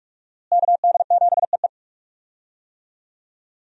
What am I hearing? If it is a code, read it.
Morse code, heard as a keyed tone, about 47 wpm, 700 Hz. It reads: KD7EE
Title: USA Call Signs - 5 Characters - Repeated 2x - 45wpm